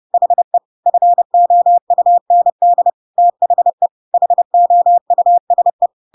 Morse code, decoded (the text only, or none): HE FOUND THE HOUSE